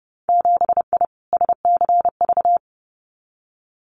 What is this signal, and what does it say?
Morse code, heard as a keyed tone, about 30 wpm, 700 Hz. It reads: 7I SCV
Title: Winter Field Day Exchanges Encore 30wpm